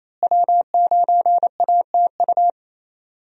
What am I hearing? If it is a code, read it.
Morse code, heard as a keyed tone, about 27 wpm, 700 Hz. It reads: W9ATU
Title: USA Call Signs - 5 Characters - Repeated 3x - 28wpm